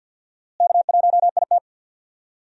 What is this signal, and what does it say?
Morse code, heard as a keyed tone, about 52 wpm, 700 Hz. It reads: K1IT
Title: USA Call Signs - 4 Characters - Repeated 2x - 50wpm